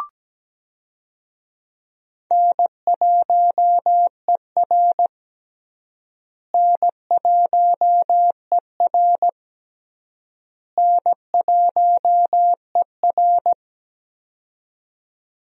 Morse code, heard as a keyed tone, about 17 wpm, 700 Hz. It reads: N1ER N1ER N1ER
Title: USA Call Signs - 4 Characters - Repeated 3x - 17wpm